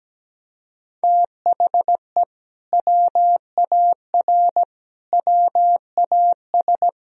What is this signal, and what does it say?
Morse code, heard as a keyed tone, about 17 wpm, 700 Hz. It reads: THE WAR WAS
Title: Sentences from Top 1000 Words - Review Part 2 - 17wpm